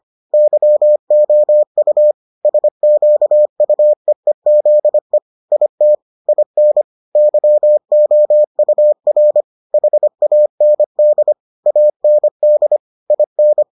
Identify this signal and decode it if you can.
Morse code, heard as a keyed tone, about 25 wpm, 600 Hz. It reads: YOU SQUEEZE IT IN YOUR HAND AND IN